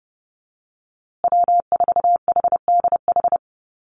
Morse code, heard as a keyed tone, about 30 wpm, 700 Hz. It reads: W4HDH
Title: USA Call Signs Repeated 2x 30wpm